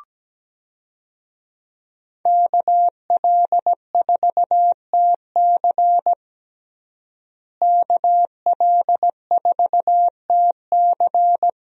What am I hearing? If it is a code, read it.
Morse code, heard as a keyed tone, about 17 wpm, 700 Hz. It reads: KL4TC KL4TC
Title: USA Call Signs - 5 Characters - Repeated 3x - 17wpm